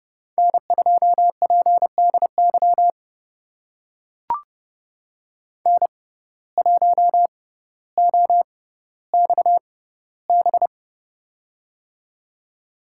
Morse code, heard as a keyed tone, about 30 wpm, spaced as for 15 wpm, 700 Hz. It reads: N2PDY N1OXB
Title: USA Call Signs - 5 Characters - 6x Character Spacing - 30wpm